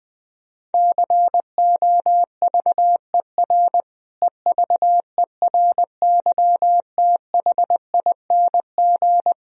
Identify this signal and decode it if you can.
Morse code, heard as a keyed tone, about 20 wpm, 700 Hz. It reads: COVER EVERYTHING
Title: Sentences from Top 1000 Words - Review Part 1 - 20wpm